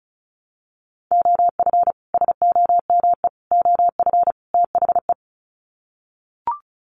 Morse code, heard as a keyed tone, about 34 wpm, 700 Hz. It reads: OF SOME OF THE
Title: Sets of 4 Words 35wpm